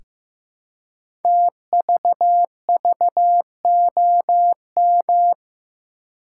Morse code, heard as a keyed tone, about 15 wpm, 700 Hz. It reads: TVVOM
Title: Code Groups - 5 Letters - 15wpm